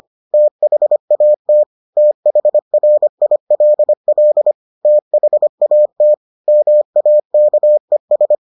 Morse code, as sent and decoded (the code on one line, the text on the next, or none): - .... .- - / - .... .-. .. .-.. .-.. / - .... .- - / -- .- -.- . ...
THAT THRILL THAT MAKES